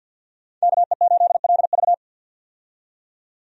Morse code, heard as a keyed tone, about 53 wpm, 700 Hz. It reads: KE8LV
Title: USA Call Signs - 5 Characters - 50wpm